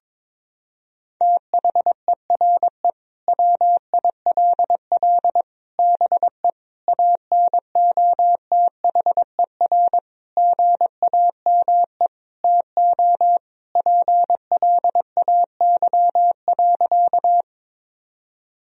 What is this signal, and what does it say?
Morse code, heard as a keyed tone, about 22 wpm, 700 Hz. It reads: THERE WILL BE ANOTHER GAME TO PLAY.
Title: Sentences from Top 1000 Words - Review Part 3 - 22wpm